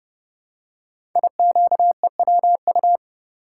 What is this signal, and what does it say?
Morse code, heard as a keyed tone, about 28 wpm, 690 Hz. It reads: IQEWU
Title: Code Groups - 5 Letters - 30wpm